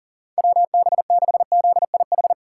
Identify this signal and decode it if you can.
Morse code, heard as a keyed tone, about 39 wpm, 700 Hz. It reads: WB6ZIH
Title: USA Call Signs - 6 Characters - 2x Character Spacing - 40wpm